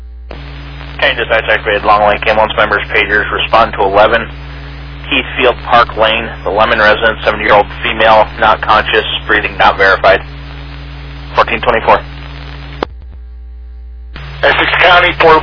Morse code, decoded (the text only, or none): none